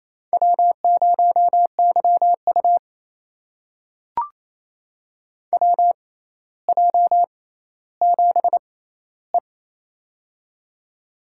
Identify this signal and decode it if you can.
Morse code, heard as a keyed tone, about 28 wpm, spaced as for 14 wpm, 700 Hz. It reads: W0YU WJ7E
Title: USA Call Signs - 4 Characters - 6x Character Spacing - 28wpm